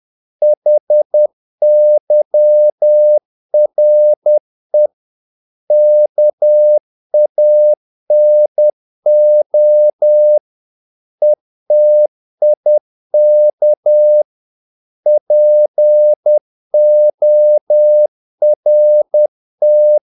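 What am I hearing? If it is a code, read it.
Morse code, heard as a keyed tone, about 10 wpm, 600 Hz. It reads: HYRE KANO ETIK PORT
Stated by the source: Morse practice of short Danish words at 10 wpm, from CW med Gnister.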